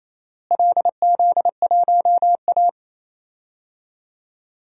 Morse code, heard as a keyed tone, about 28 wpm, 700 Hz. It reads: LZ1A
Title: DXCC-100 Call Signs - 4 Characters - Repeated 3x - 28wpm